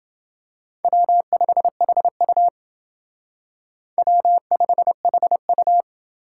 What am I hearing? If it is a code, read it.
Morse code, heard as a keyed tone, about 28 wpm, 700 Hz. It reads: W5HU W5HU